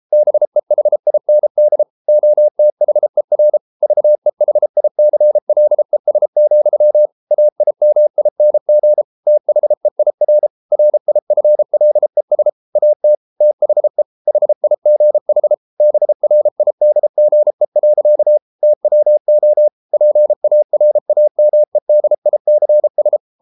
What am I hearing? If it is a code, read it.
Morse code, heard as a keyed tone, about 33 wpm, 600 Hz. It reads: BEHIND OTHER VEHICLES, AIMING THEIR RIFLES AT THE HIGH BRIDGE. TWO PARAMEDICS